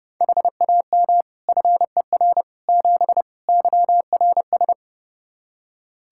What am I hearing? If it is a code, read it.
Morse code, heard as a keyed tone, about 30 wpm, 700 Hz. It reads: HAM FER 7 YRS
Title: Practice QSO - Exchange 3 - More Info - 30wpm